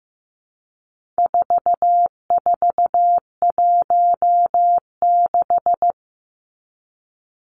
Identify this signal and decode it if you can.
Morse code, heard as a keyed tone, about 15 wpm, 700 Hz. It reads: 4416
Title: Numbers - 4-Digit 15wpm